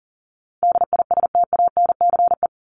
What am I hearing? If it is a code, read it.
Morse code, heard as a keyed tone, about 40 wpm, 700 Hz. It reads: DISTANCE
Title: Top 1001-1250 Words 40wpm